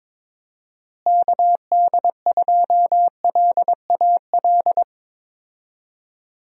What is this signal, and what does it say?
Morse code, heard as a keyed tone, about 22 wpm, 700 Hz. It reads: KD2LAL
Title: USA Call Signs - 6 Characters - Repeated 2x - 22wpm